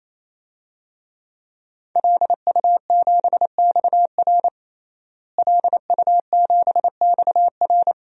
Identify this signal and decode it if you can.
Morse code, heard as a keyed tone, about 28 wpm, 700 Hz. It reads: LU7XR LU7XR
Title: DXCC-100 Call Signs - 5 Characters - Repeated 3x - 28wpm